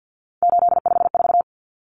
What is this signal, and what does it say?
Morse code, heard as a keyed tone, about 50 wpm, 700 Hz. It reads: 854
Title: Common Cut Numbers - 3-Digit 50wpm